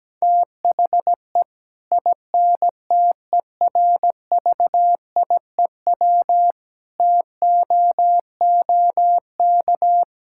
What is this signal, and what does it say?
Morse code, heard as a keyed tone, about 17 wpm, 700 Hz. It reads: THE INTERVIEW TOOK
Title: Sentences from Top 1600 Words 17wpm